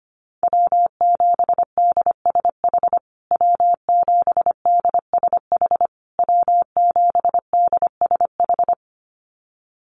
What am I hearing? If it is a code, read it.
Morse code, heard as a keyed tone, about 25 wpm, 700 Hz. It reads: W7DSH W7DSH W7DSH
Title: USA Call Signs Repeated 3x 25wpm